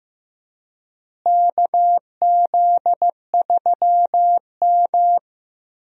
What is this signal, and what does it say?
Morse code, heard as a keyed tone, about 15 wpm, 700 Hz. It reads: KZ3M
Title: USA Call Signs - 4 Characters - Repeated 2x - 15wpm